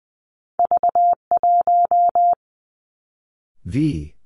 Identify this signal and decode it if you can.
Morse code, heard as a keyed tone, about 19 wpm, 700 Hz. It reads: V 1
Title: Two Challenging Letter-Number Combinations 20wpm